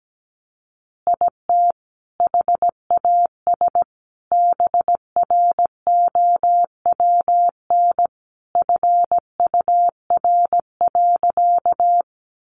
Morse code, heard as a keyed tone, about 17 wpm, 700 Hz. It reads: IT HAS BROWN FUR.
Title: Easy Phrases 17wpm